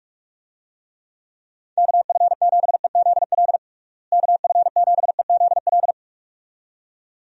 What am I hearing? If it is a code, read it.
Morse code, heard as a keyed tone, about 46 wpm, 700 Hz. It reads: KF7EZL KF7EZL
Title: USA Call Signs - 6 Characters - Repeated 2x - 45wpm